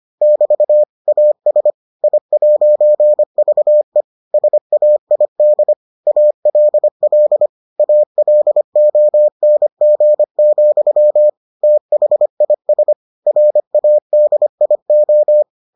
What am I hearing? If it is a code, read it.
Morse code, heard as a keyed tone, about 25 wpm, 600 Hz. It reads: <BT> AS I'VE SAID ALL ALONG, THIS RADIO